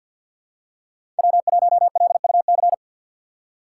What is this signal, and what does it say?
Morse code, heard as a keyed tone, about 51 wpm, 700 Hz. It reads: W1LUC
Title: USA Call Signs - 5 Characters - 2x Character Spacing - 50wpm